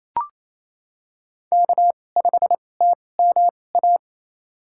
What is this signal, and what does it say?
Morse code, heard as a keyed tone, about 28 wpm, spaced as for 20 wpm, 700 Hz. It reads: K5TMA
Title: USA Call Signs - 2x Character Spacing - 28wpm